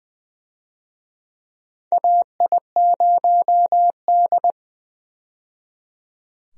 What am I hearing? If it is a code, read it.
Morse code, heard as a keyed tone, about 21 wpm, 700 Hz. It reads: AI0D
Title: USA Call Signs - 4 Characters - 20wpm